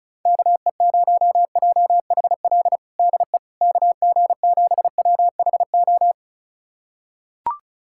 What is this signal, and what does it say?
Morse code, heard as a keyed tone, about 36 wpm, 700 Hz. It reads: KE0JHL DE KG7WHO
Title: USA Callsign DE Callsign 35wpm